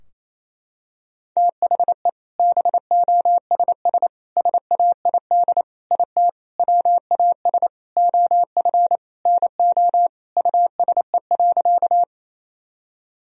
Morse code, heard as a keyed tone, about 28 wpm, 700 Hz. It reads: THE BOSS SAID IT WAS OF NO USE.
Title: Sentences from Top 1000 Words - Review Part 4 - 28wpm